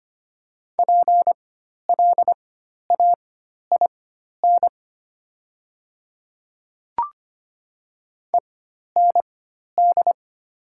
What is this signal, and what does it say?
Morse code, heard as a keyed tone, about 25 wpm, spaced as for 12 wpm, 690 Hz. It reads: PLAIN END
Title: 4x Character Spacing - Top 500 Words - 25wpm